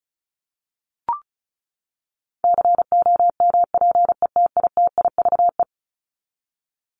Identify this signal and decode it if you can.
Morse code, heard as a keyed tone, about 34 wpm, 700 Hz. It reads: COMPETITIVE
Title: Top 1001-1250 Words 35wpm